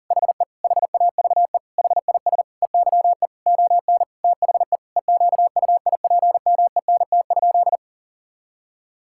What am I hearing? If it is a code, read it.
Morse code, heard as a keyed tone, about 40 wpm, 700 Hz. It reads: HE HAVE HIS EYE ON THE EQUIPMENT?
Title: Sentences from Top 1000 Words - Review Part 5 - 40wpm